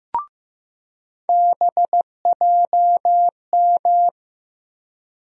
Morse code, heard as a keyed tone, about 15 wpm, 700 Hz. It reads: BJM
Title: USA Call Sign Postfixes - 3 Letters 15wpm